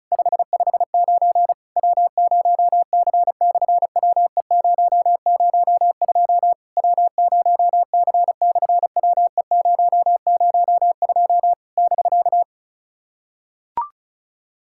Morse code, heard as a keyed tone, about 33 wpm, 700 Hz. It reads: 559 W0C/WE002 W0C/WE002 <BK>
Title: SOTA QSO Practice 35wpm